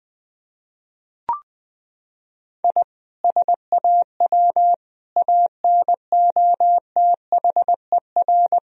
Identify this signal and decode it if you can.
Morse code, heard as a keyed tone, about 19 wpm, 700 Hz. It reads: I SAW ANOTHER
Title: Sentences from Top 1000 Words - Review Part 3 - 20wpm